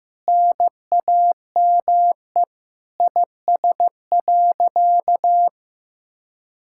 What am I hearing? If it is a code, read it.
Morse code, heard as a keyed tone, about 15 wpm, 700 Hz. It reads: NAME IS.
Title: Sentences from Top 1000 Words - Review Part 6 - 15wpm